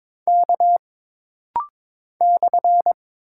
Morse code, heard as a keyed tone, about 22 wpm, 700 Hz. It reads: K /
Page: This rapid-fire timing is geared to push your ICR (Instant Character Recognition) skills.